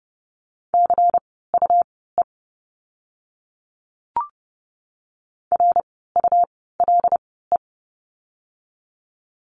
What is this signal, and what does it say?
Morse code, heard as a keyed tone, about 30 wpm, spaced as for 17 wpm, 700 Hz. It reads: CUE RULE
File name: Lesson 17f - Letter u - Words 30wpm